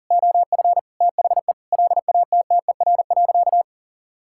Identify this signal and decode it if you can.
Morse code, heard as a keyed tone, about 40 wpm, 700 Hz. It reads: OF THE LATTER.
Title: Sentences from Top 1600 Words 40wpm